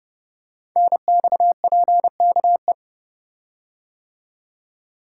Morse code, heard as a keyed tone, about 30 wpm, 700 Hz. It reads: NXPKE
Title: Code Groups - 5 Letters - 30wpm